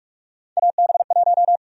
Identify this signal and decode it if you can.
Morse code, heard as a keyed tone, about 42 wpm, 700 Hz. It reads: AB1
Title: USA Call Sign Prefixes - 45wpm